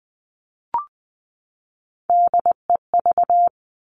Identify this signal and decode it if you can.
Morse code, heard as a keyed tone, about 19 wpm, 700 Hz. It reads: DEV
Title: Three Challenging Letter-Number Combinations 20wpm